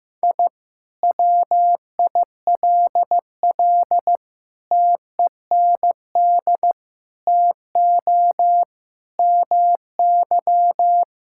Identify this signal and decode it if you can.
Morse code, heard as a keyed tone, about 14 wpm, 700 Hz. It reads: I WILL TEND TO MY
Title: Sentences from Top 1000 Words - Review Part 5 - 15wpm